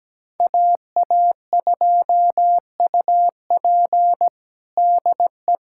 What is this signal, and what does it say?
Morse code, heard as a keyed tone, about 17 wpm, 700 Hz. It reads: AA2UP DE
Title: USA Callsign DE Callsign - 5 Characters - 17wpm